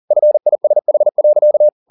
Morse code, heard as a keyed tone, about 40 wpm, 600 Hz. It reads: FISH.
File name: the old man and the sea 2 40wpm